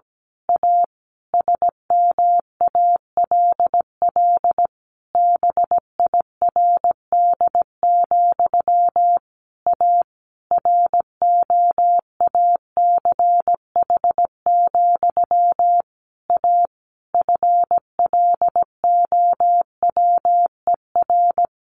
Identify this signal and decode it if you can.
Morse code, heard as a keyed tone, about 17 wpm, 700 Hz. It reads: A SMALL BIRD, A ROACH, A FLOWER
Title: A Sound Of Thunder 17wpm